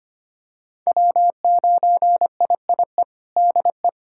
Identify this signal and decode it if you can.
Morse code, heard as a keyed tone, about 25 wpm, 700 Hz. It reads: W9IIE DE
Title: USA Callsign DE Callsign - 5 Characters - 25wpm